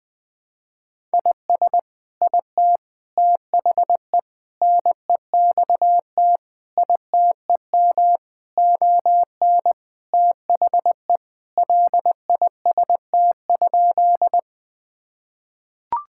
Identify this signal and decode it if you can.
Morse code, heard as a keyed tone, about 20 wpm, 700 Hz. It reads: IS IT THE NEXT ITEM ON THE LIST?
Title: Sentences from Top 1000 Words - Review Part 6 - 20wpm